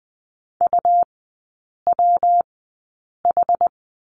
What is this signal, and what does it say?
Morse code, heard as a keyed tone, about 20 wpm, spaced as for 8 wpm, 700 Hz. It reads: UWH
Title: 121 Lesson 20e - Number 3 - Three Characters 20wpm